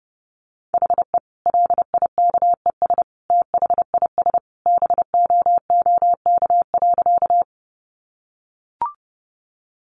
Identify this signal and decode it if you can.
Morse code, heard as a keyed tone, about 30 wpm, 700 Hz. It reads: HE LIKES THIS BOOK.
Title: Additional Easy Phrases 30wpm